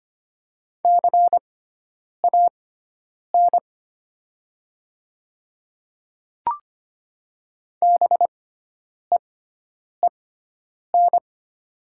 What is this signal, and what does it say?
Morse code, heard as a keyed tone, about 25 wpm, spaced as for 9 wpm, 700 Hz. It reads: CAN BEEN
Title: 6x Character Spacing - Top 100 Words - 25wpm